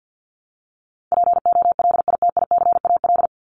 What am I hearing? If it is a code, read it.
Morse code, heard as a keyed tone, about 50 wpm, 700 Hz. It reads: POLITICAL
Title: Top 501-750 Words 50wpm